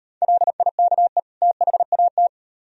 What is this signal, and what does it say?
Morse code, heard as a keyed tone, about 38 wpm, 700 Hz. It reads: LIKE THAT